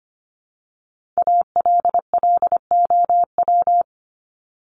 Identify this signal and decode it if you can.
Morse code, heard as a keyed tone, about 25 wpm, 700 Hz. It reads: ALLOW
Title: Top 501-750 Words 25wpm